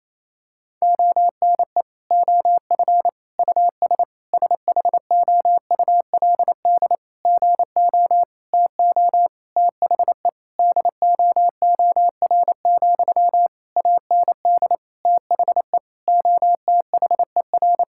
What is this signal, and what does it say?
Morse code, heard as a keyed tone, about 28 wpm, 700 Hz. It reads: ONE OF US SHOULD GO TO THE DOOR, AND THE OTHER
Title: Sentences from Top 1600 Words 28wpm